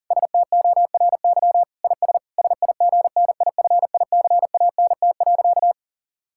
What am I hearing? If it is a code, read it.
Morse code, heard as a keyed tone, about 40 wpm, 700 Hz. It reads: STORY IS SIGNIFICANT.
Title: Sentences from Top 1000 Words - Review Part 6 - 40wpm